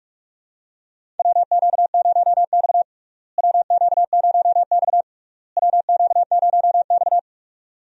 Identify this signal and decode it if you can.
Morse code, heard as a keyed tone, about 47 wpm, 700 Hz. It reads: WQ0X WQ0X WQ0X
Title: USA Call Signs - 4 Characters - Repeated 3x - 45wpm